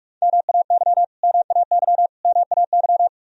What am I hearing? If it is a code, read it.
Morse code, heard as a keyed tone, about 46 wpm, 700 Hz. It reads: MAY MAY MAY
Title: Top 100 Words Repeated 45wpm